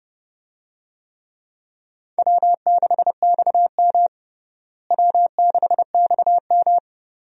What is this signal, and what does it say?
Morse code, heard as a keyed tone, about 30 wpm, 700 Hz. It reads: W6XM W6XM
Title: USA Call Signs - 4 Characters - Repeated 2x - 30wpm